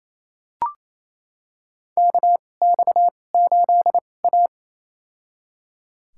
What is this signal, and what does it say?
Morse code, heard as a keyed tone, about 28 wpm, spaced as for 20 wpm, 700 Hz. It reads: KX8A
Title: USA Call Signs - 4 Characters - 2x Character Spacing - 28wpm